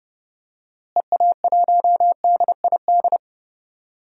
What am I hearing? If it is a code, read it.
Morse code, heard as a keyed tone, about 30 wpm, 700 Hz. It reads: EA1DID
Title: DXCC-100 Call Signs - 6 Characters - Repeated 2x - 30wpm